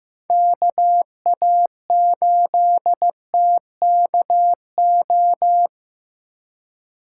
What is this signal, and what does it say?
Morse code, heard as a keyed tone, about 18 wpm, 700 Hz. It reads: KA8TKO
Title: Speed Racing - USA 2x3 Call Signs 15wpm